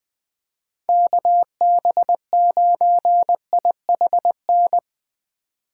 Morse code, heard as a keyed tone, about 20 wpm, 700 Hz. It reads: KB9IHN
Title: Speed Racing - USA 2x3 Call Signs 17wpm